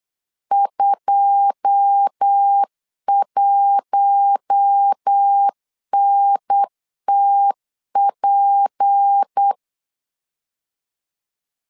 Morse code, heard as a keyed tone, about 8 wpm, 800 Hz. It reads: 21NTP
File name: LEZIONI / 9 2867 8wpm 3440